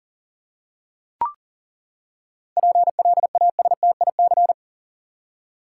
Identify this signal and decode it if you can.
Morse code, heard as a keyed tone, about 38 wpm, 700 Hz. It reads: PLASTIC
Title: Seven Letter Words - 40wpm